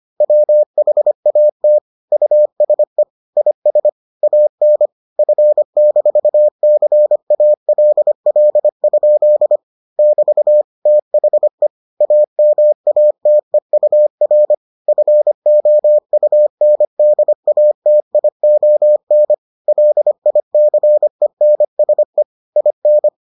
Morse code, heard as a keyed tone, about 25 wpm, 600 Hz. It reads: WHAT USE IS AN F-CALL? <BT> THE AMATEUR FOUNDATION LICENSE IN